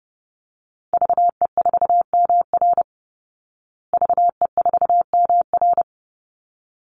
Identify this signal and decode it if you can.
Morse code, heard as a keyed tone, about 28 wpm, 700 Hz. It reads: VE4MR VE4MR
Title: Call Signs Repeated 3x - 30wpm